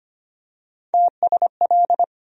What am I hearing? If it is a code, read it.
Morse code, heard as a keyed tone, about 25 wpm, 700 Hz. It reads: TSL
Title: USA Call Sign Postfixes - 3 Letters 25wpm